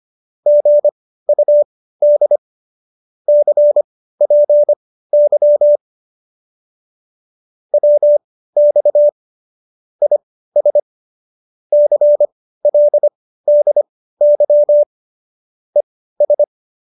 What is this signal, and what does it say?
Morse code, heard as a keyed tone, about 25 wpm, spaced as for 15 wpm, 590 Hz. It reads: GUD CPY WX IS CLDY ES